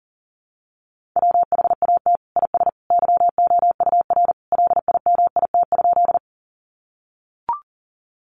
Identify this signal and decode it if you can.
Morse code, heard as a keyed tone, about 40 wpm, 700 Hz. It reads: WHAT IS YOUR LIMIT?
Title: Sentences from Top 1500 Words 40wpm